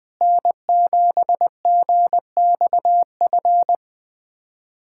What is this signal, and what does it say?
Morse code, heard as a keyed tone, about 21 wpm, 700 Hz. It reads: N7GXF
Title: USA Callsign DE Callsign - 5 Characters - 20wpm